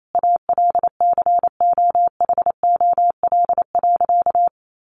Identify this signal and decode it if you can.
Morse code, heard as a keyed tone, about 28 wpm, 700 Hz. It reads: ALCOHOL.
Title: Sentences from Top 1400 Words 28wpm